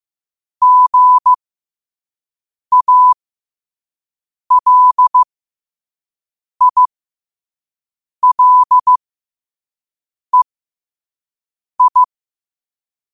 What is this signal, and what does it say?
Morse code, heard as a keyed tone, about 15 wpm, spaced as for 5 wpm, 1000 Hz. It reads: GALILEI